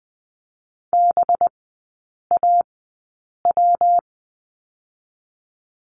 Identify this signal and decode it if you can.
Morse code, heard as a keyed tone, about 20 wpm, spaced as for 8 wpm, 700 Hz. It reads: BAW
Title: 212 Lesson 34e - Number 8 - Three Characters 20wpm